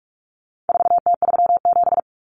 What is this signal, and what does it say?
Morse code, heard as a keyed tone, about 44 wpm, 700 Hz. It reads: 4T37
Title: Common Cut Numbers - 4-Digit 45wpm